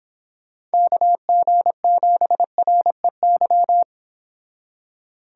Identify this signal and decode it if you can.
Morse code, heard as a keyed tone, about 30 wpm, 700 Hz. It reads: KG7REY